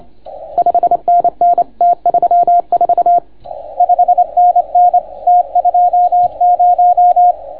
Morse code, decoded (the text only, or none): 5NN T34 5NN T20